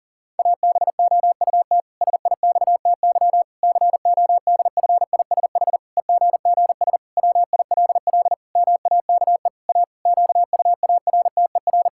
Morse code, heard as a keyed tone, about 40 wpm, 700 Hz. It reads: ABOUT SIXTY CODFISH EGGS WILL MAKE A QUARTER
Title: Pangram Challenge v2 - 40wpm